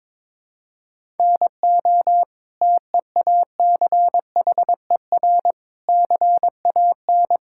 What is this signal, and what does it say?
Morse code, heard as a keyed tone, about 21 wpm, 700 Hz. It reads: NO TEACHER CAN
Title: Sentences from Top 1000 Words - Review Part 6 - 22wpm